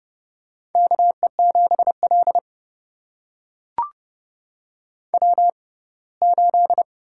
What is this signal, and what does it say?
Morse code, heard as a keyed tone, about 30 wpm, spaced as for 20 wpm, 700 Hz. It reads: KE7L W8
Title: USA Call Signs - 4 Characters - 6x Character Spacing - 30wpm